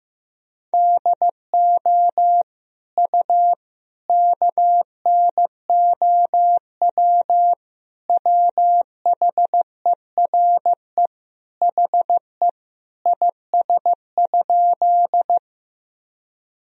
Morse code, heard as a keyed tone, about 15 wpm, 700 Hz. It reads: DO U KNOW WHERE HE IS?
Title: Sentences from Top 300 Words and Abbreviations 15wpm